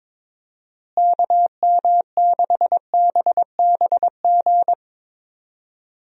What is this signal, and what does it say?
Morse code, heard as a keyed tone, about 21 wpm, 700 Hz. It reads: KM6BBG
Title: USA Call Signs - 6 Characters - Repeated 2x - 22wpm